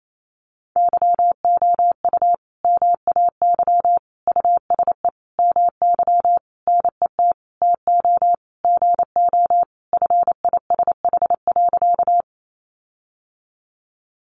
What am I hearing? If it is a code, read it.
Morse code, heard as a keyed tone, about 28 wpm, 700 Hz. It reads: YOU MAY USE MY NET TO GO FISH.
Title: Sentences from Top 1400 Words 28wpm